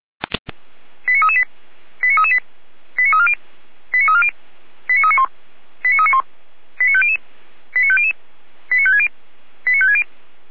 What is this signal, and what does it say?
Five-tone selective call (ZVEI), heard as quick runs of five tones, tones of about 70 ms. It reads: 89338 89338 89340 89340 89391 89391 89500 89500 89560 89560
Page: /455/SDR/ZVEI/89336/